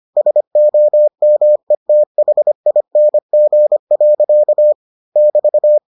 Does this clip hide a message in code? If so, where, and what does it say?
Yes: Morse code from 0.2 to 5.8 s, heard as a keyed tone, about 25 wpm, 600 Hz. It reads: SOMETHING. <BT>